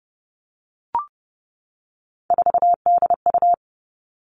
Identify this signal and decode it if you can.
Morse code, heard as a keyed tone, about 30 wpm, 700 Hz. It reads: 4DU
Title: All Cut Numbers - 3-Digit 30wpm